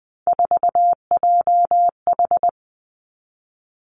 Morse code, heard as a keyed tone, about 20 wpm, 700 Hz. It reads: 4JH
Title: Three Challenging Letter-Number Combinations 20wpm